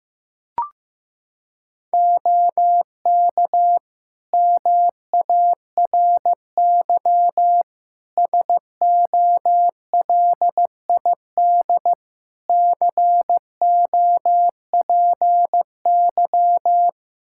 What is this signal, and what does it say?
Morse code, heard as a keyed tone, about 15 wpm, 700 Hz. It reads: OK MARY SOLID COPY
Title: Practice QSO - Exchange 3 - More Info - 15wpm